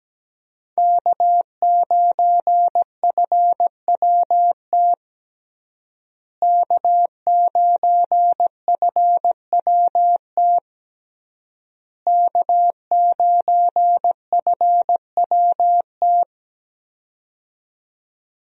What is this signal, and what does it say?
Morse code, heard as a keyed tone, about 17 wpm, 700 Hz. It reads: K9FWT K9FWT K9FWT
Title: USA Call Signs - 5 Characters - Repeated 3x - 17wpm